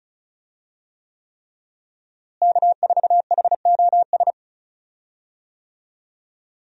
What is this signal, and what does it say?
Morse code, heard as a keyed tone, about 34 wpm, 700 Hz. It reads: K4HOS
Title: USA Call Signs - 5 Characters - Repeated 3x - 35wpm